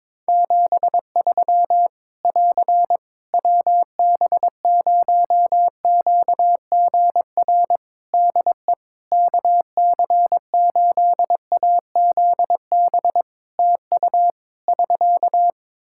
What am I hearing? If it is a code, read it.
Morse code, heard as a keyed tone, about 22 wpm, 700 Hz. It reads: 73 <AR> WB0QGR DE KC8AZB TU <SK>
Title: Practice QSO - Exchange 4 - Ending - 22wpm